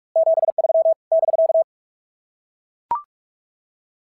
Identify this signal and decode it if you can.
Morse code, heard as a keyed tone, about 44 wpm, 690 Hz. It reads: 73 <BK>
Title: SOTA QSO Practice 45wpm